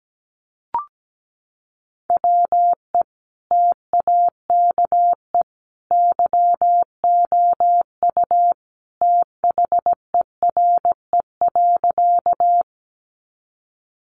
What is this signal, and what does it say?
Morse code, heard as a keyed tone, about 17 wpm, 700 Hz. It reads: WE TAKE YOU THERE.
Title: A Sound Of Thunder 17wpm